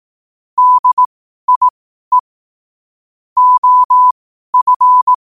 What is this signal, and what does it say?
Morse code, heard as a keyed tone, about 18 wpm, spaced as for 12 wpm, 1000 Hz. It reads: DIE OF
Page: Quotes for Sun, 27 Apr 2025 in Morse Code at 12 words per minute.